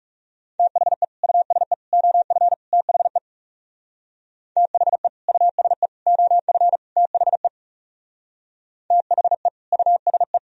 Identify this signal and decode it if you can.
Morse code, heard as a keyed tone, about 39 wpm, 700 Hz. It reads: THE USE OF THE THE USE OF THE THE USE